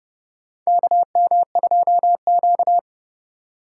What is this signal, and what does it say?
Morse code, heard as a keyed tone, about 30 wpm, 700 Hz. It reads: KM2Q
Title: USA Call Signs - 4 Characters - 4x Character Spacing - 30wpm